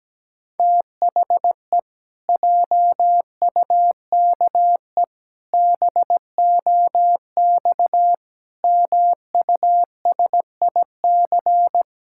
Morse code, heard as a keyed tone, about 17 wpm, 700 Hz. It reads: THE JUKE BOX MUSIC
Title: Pangram Challenge v2 - 17wpm